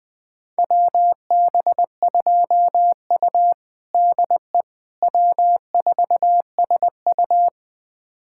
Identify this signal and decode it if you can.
Morse code, heard as a keyed tone, about 21 wpm, 700 Hz. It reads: WB2U DE W4SU
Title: USA Callsign DE Callsign - 4 Characters - 20wpm